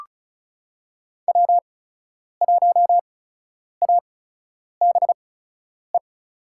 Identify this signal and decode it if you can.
Morse code, heard as a keyed tone, about 36 wpm, spaced as for 10 wpm, 700 Hz. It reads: W1ABE
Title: USA Call Signs - 8x Character Spacing - 35wpm